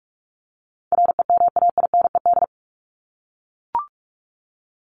Morse code, heard as a keyed tone, about 44 wpm, 700 Hz. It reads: REMAINED
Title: ED Suffix - Top 100 Words with Base Words 45wpm